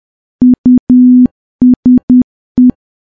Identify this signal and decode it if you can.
Morse code, heard as a keyed tone, about 10 wpm, 250 Hz. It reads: USE
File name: morse / morse-training-10WPM